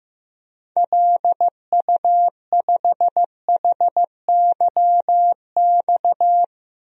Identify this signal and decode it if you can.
Morse code, heard as a keyed tone, about 16 wpm, 700 Hz. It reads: LU5HYX
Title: DXCC-100 Call Signs - 6 Characters - 15wpm